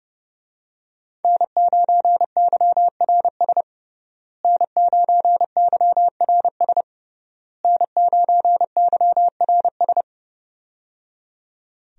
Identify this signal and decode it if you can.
Morse code, heard as a keyed tone, about 30 wpm, 700 Hz. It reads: N9YRS N9YRS N9YRS
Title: USA Call Signs - 5 Characters - Repeated 3x - 30wpm